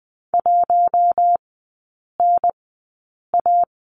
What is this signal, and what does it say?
Morse code, heard as a keyed tone, about 20 wpm, spaced as for 8 wpm, 700 Hz. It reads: 1NA
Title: 067 Lesson 12e - Letter d - Three Characters 20wpm